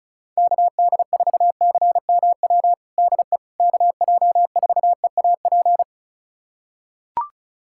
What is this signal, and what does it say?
Morse code, heard as a keyed tone, about 36 wpm, 700 Hz. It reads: KD4CMW DE KJ4EAP
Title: USA Callsign DE Callsign 35wpm